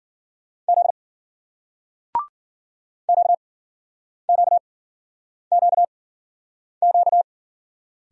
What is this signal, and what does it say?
Morse code, heard as a keyed tone, about 51 wpm, spaced as for 9 wpm, 700 Hz. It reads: Z QQQQ